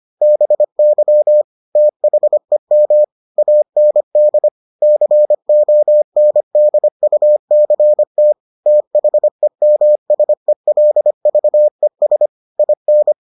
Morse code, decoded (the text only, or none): BY THEM AND CONDUCT THEMSELVES IN